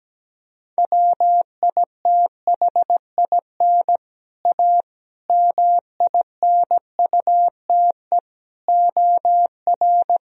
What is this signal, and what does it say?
Morse code, heard as a keyed tone, about 17 wpm, 700 Hz. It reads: WITHIN A MINUTE OR
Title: Sentences from Top 1000 Words - Review Part 4 - 17wpm